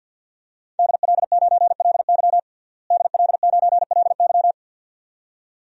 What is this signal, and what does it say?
Morse code, heard as a keyed tone, about 51 wpm, 700 Hz. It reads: DL9LY DL9LY
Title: DXCC-100 Call Signs - 5 Characters - Repeated 2x - 50wpm